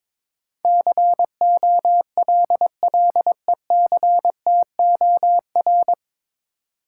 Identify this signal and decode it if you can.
Morse code, heard as a keyed tone, about 22 wpm, 700 Hz. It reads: COLLECTOR
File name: 9 Letter Words - First 3 Letters Given - 22wpm